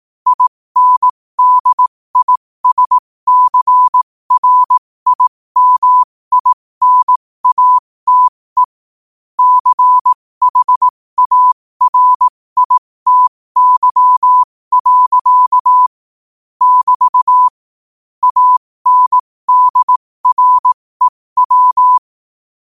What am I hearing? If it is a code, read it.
Morse code, heard as a keyed tone, about 18 wpm, spaced as for 15 wpm, 1000 Hz. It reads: INDISCRIMINATE CHARITY. <BT> ANDREW
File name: qotdfiles / 15wpm